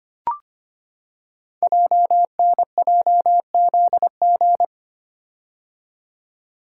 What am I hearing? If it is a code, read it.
Morse code, heard as a keyed tone, about 25 wpm, 700 Hz. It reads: JNJZG
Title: Code Groups - 5 Letters - 25wpm